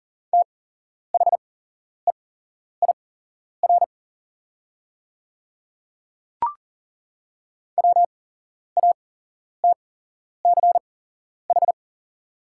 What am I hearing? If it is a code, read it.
Morse code, heard as a keyed tone, about 38 wpm, spaced as for 11 wpm, 700 Hz. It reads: THEIR WATCH
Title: 8x Character Spacing - Top 300 Words - 40wpm